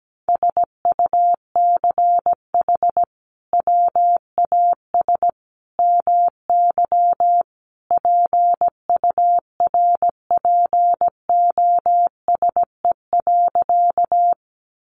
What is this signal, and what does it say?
Morse code, heard as a keyed tone, about 17 wpm, 700 Hz. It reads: SUCH WAS MY PURPOSE.
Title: Sentences from Top 800 Words 17wpm